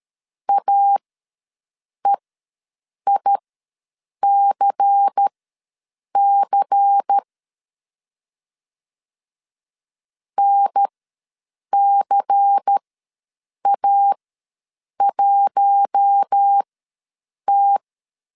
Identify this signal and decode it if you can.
Morse code, heard as a keyed tone, about 13 wpm, spaced as for 7 wpm, 800 Hz. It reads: AEICC NCA1T
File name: LEZIONI / 5 LBVC / 5 LBVC ws 12wpm 3159